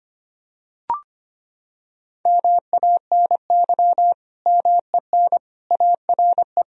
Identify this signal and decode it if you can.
Morse code, heard as a keyed tone, about 25 wpm, 700 Hz. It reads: MANY MEN ARE
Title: Sentences from Top 1000 Words - Review Part 3 - 25wpm